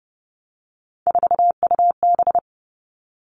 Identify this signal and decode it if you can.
Morse code, heard as a keyed tone, about 30 wpm, 700 Hz. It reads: 4UB
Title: All Cut Numbers - 3-Digit 30wpm